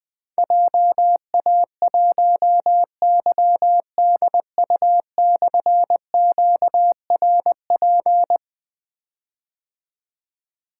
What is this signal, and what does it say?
Morse code, heard as a keyed tone, about 20 wpm, 700 Hz. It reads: JA1YDU/QRP
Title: Difficult Call Signs 20wpm